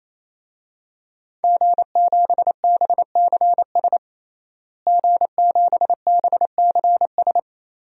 Morse code, heard as a keyed tone, about 28 wpm, 700 Hz. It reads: G7BCS G7BCS
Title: DXCC-100 Call Signs - 5 Characters - Repeated 3x - 28wpm